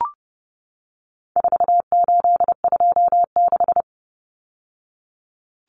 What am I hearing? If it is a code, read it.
Morse code, heard as a keyed tone, about 30 wpm, 700 Hz. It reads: 4826
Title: Numbers - 4-Digit 30wpm